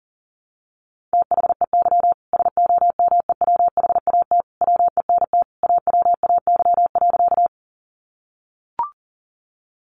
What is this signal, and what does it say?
Morse code, heard as a keyed tone, about 40 wpm, 700 Hz. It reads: THEY SOMEWHAT WENT AWAY.
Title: Sentences from Top 1300 Words 40wpm